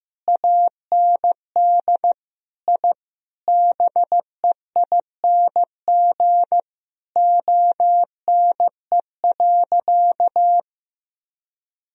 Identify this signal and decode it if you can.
Morse code, heard as a keyed tone, about 15 wpm, 700 Hz. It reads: AND I BEING ONE.
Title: Sentences from Top 1000 Words - Review Part 3 - 15wpm